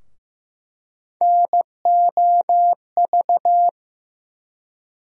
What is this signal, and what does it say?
Morse code, heard as a keyed tone, about 15 wpm, 700 Hz. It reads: NOV
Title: USA Call Sign Postfixes - 3 Letters 15wpm